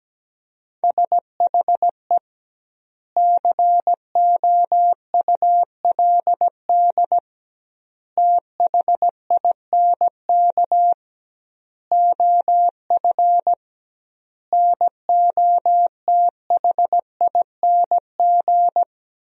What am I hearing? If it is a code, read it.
Morse code, heard as a keyed tone, about 17 wpm, 700 Hz. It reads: SHE COULD THINK OF NOTHING
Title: 2x Word-Spacing - Sentences from Top 500 Words - 17wpm